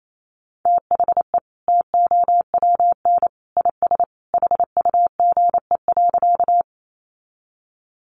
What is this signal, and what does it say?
Morse code, heard as a keyed tone, about 28 wpm, 700 Hz. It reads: THE TOWN IS HUGE.
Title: Sentences from Top 1100 Words 28wpm